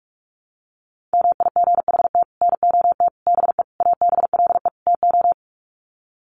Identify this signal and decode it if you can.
Morse code, heard as a keyed tone, about 44 wpm, 700 Hz. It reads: MIGHT NOT BE ABLE TO
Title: Sets of 5 Words Encore2 45wpm